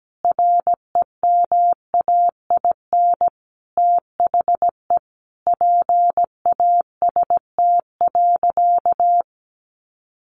Morse code, heard as a keyed tone, about 17 wpm, 700 Hz. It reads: REMAIN THE PAST.
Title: Sentences from Top 1200 Words 17wpm